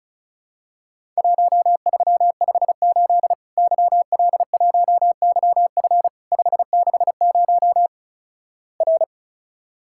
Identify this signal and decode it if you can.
Morse code, heard as a keyed tone, about 36 wpm, 700 Hz. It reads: 1358 YL1YF 560 R
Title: Worked All Europe DX Contest - QTC Exchange - 35wpm